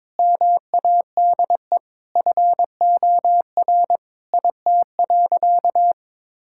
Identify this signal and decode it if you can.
Morse code, heard as a keyed tone, about 22 wpm, 700 Hz. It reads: MADE FOR IT.
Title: Sentences from Top 1600 Words 22wpm